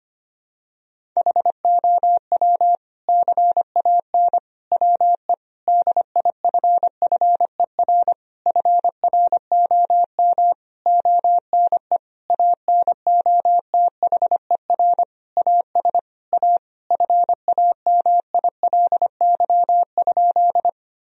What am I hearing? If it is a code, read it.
Morse code, heard as a keyed tone, about 25 wpm, 700 Hz. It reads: HOW CAN WE DIFFER FROM ONE ANOTHER AS A FAMILY?
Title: Sentences from Top 1000 Words - Review Part 1 - 25wpm